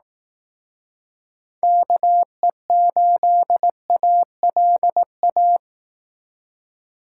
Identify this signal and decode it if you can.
Morse code, heard as a keyed tone, about 18 wpm, 700 Hz. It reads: KE8ALA
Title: Speed Racing - USA 2x3 Call Signs 15wpm